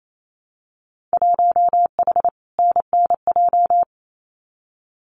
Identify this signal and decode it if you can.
Morse code, heard as a keyed tone, about 28 wpm, 700 Hz. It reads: 1H NNJ
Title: Winter Field Day Exchanges 28wpm